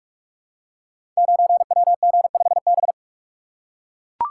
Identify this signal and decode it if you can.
Morse code, heard as a keyed tone, about 44 wpm, 700 Hz. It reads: 9WG5B
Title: Code Groups - 5 Letter:Numbers - 45wpm